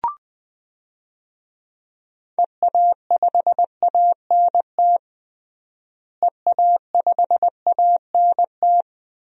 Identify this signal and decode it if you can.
Morse code, heard as a keyed tone, about 20 wpm, 700 Hz. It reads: EA5ANT EA5ANT
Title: DXCC-100 Call Signs - 6 Characters - Repeated 2x - 20wpm